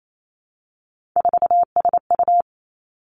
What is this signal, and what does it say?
Morse code, heard as a keyed tone, about 28 wpm, 700 Hz. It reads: 4SU
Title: Three Challenging Letter-Number Combinations 28wpm